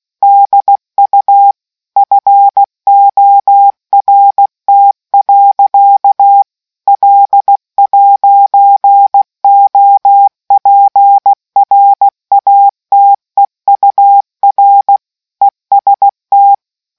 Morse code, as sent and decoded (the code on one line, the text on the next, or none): -.. ..- / ..-. --- .-. - .-.-.- / .-.. .----. --- .--. .-. .- - . ..- .-. / . ... -
DU FORT. L'OPRATEUR EST